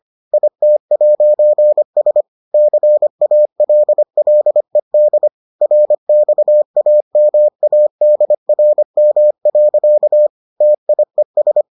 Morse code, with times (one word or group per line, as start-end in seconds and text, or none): 0.3-2.2 IT'S
2.5-5.3 CALLED
5.6-10.3 RXAMADRM.
10.6-11.6 TIES